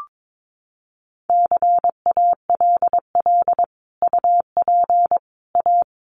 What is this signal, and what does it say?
Morse code, heard as a keyed tone, about 22 wpm, 700 Hz. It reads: CALL UP A
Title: Sentences from Top 1500 Words 22wpm